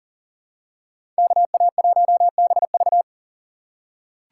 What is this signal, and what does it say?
Morse code, heard as a keyed tone, about 37 wpm, 700 Hz. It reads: KA1BV
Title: USA Call Signs - 5 Characters - 2x Character Spacing - 40wpm